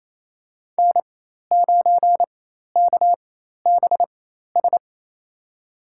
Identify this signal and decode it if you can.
Morse code, heard as a keyed tone, about 27 wpm, spaced as for 13 wpm, 700 Hz. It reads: N9KBS
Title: USA Call Signs - 5 Characters - 4x Character Spacing - 28wpm